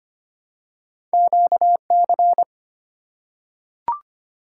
Morse code, heard as a keyed tone, about 25 wpm, 700 Hz. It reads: QC
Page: morse-code-ninja-practice-25wpm